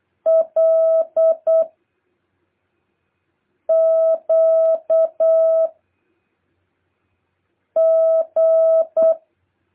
Morse code, heard as a keyed tone, about 8 wpm, 650 Hz. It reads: L Q G
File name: jh8cbh / 2023cw